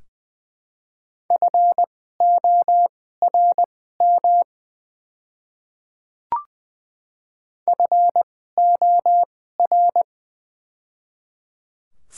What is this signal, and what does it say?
Morse code, heard as a keyed tone, about 20 wpm, spaced as for 15 wpm, 700 Hz. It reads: FORM FOR
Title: 2x Character Spacing - Top 200 Words - 20wpm